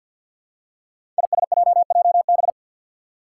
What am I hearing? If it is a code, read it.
Morse code, heard as a keyed tone, about 47 wpm, 700 Hz. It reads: ISJJB
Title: Five Challenging Letter-Number Combinations 50wpm